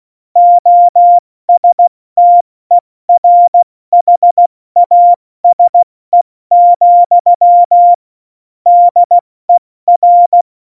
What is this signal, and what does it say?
Morse code, heard as a keyed tone, about 16 wpm, 700 Hz. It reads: OSTERHASE, DER